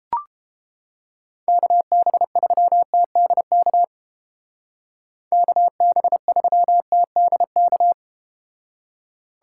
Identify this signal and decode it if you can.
Morse code, heard as a keyed tone, about 30 wpm, 700 Hz. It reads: KB3TDK KB3TDK